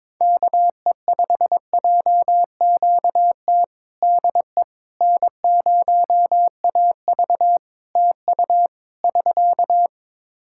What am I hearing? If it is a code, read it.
Morse code, heard as a keyed tone, about 22 wpm, 700 Hz. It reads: KE5JQT DE N0AV TU <SK>
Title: Practice QSO - Exchange 4 - Ending - 22wpm